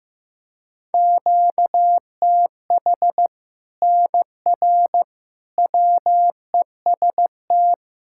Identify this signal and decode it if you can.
Morse code, heard as a keyed tone, about 15 wpm, 700 Hz. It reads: QTH NR WEST
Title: Practice QSO - Exchange 1 - Beginning - 15wpm